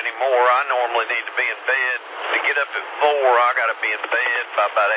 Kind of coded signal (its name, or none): Emergency Alert System (SAME) tones